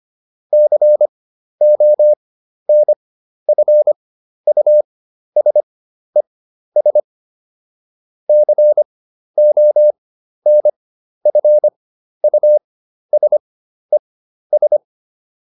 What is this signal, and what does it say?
Morse code, heard as a keyed tone, about 25 wpm, spaced as for 12 wpm, 600 Hz. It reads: CONFUSES CONFUSES